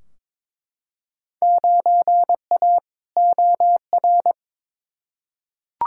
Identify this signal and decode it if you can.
Morse code, heard as a keyed tone, about 22 wpm, 700 Hz. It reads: 9A OR
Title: Field Day Exchanges Encore - 22wpm